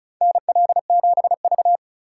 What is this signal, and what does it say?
Morse code, heard as a keyed tone, about 34 wpm, 700 Hz. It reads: NL7V
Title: K1USN SST QSO Practice - 35wpm